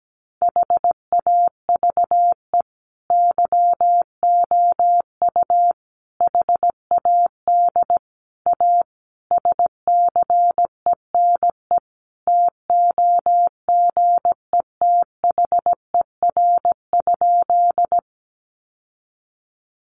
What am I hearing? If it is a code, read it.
Morse code, heard as a keyed tone, about 17 wpm, 700 Hz. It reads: HAVE YOU HAD A SCENE TOGETHER?
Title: Sentences from Top 1100 Words 17wpm